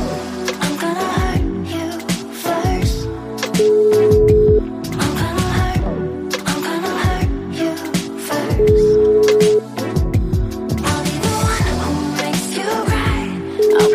Ringback tone (Europe, 425 Hz), with noise.